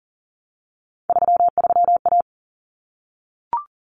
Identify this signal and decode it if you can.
Morse code, heard as a keyed tone, about 40 wpm, 700 Hz. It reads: 33A
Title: Common Cut Numbers - 3-Digit 40wpm